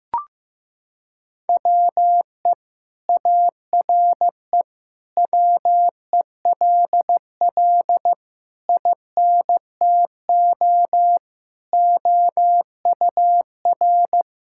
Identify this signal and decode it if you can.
Morse code, heard as a keyed tone, about 15 wpm, 700 Hz. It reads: WE ARE WELL INTO OUR
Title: Sentences from Top 1000 Words - Review Part 5 - 15wpm